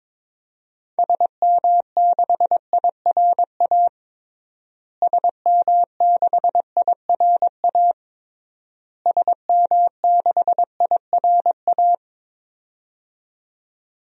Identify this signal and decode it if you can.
Morse code, heard as a keyed tone, about 22 wpm, 700 Hz. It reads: SM6IRA SM6IRA SM6IRA
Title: DXCC-100 Call Signs - 6 Characters - Repeated 3x - 22wpm